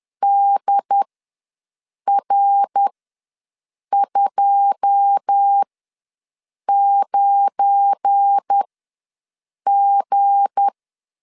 Morse code, heard as a keyed tone, about 10 wpm, spaced as for 6 wpm, 800 Hz. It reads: DR29G
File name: LEZIONI / 9 2 ws 10wpm 4130